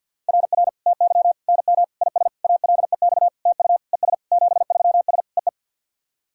Morse code, heard as a keyed tone, about 50 wpm, 690 Hz. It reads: RR TY NM IS ALEX TU ES 73S EE
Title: SOTA QSO Practice 50wpm